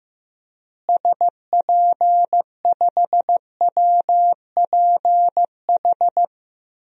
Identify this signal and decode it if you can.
Morse code, heard as a keyed tone, about 15 wpm, 700 Hz. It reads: SP5WPH
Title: DXCC-100 Call Signs - 6 Characters - Repeated 3x - 15wpm